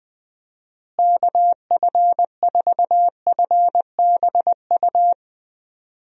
Morse code, heard as a keyed tone, about 21 wpm, 700 Hz. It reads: KF4FBU
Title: USA Call Signs - 6 Characters - Repeated 2x - 20wpm